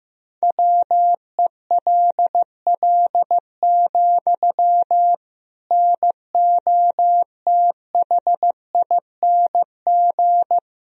Morse code, heard as a keyed tone, about 15 wpm, 700 Hz. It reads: WELL, NOTHING
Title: Sentences from Top 1000 Words - Review Part 3 - 15wpm